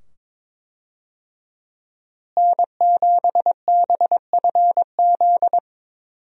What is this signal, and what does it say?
Morse code, heard as a keyed tone, about 22 wpm, 700 Hz. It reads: N7BFZ
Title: USA Call Signs - 5 Characters - Repeated 2x - 22wpm